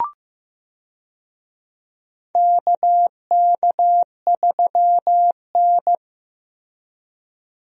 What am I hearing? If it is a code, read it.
Morse code, heard as a keyed tone, about 15 wpm, 700 Hz. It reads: KK3N
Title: USA Call Signs - 4 Characters - 15wpm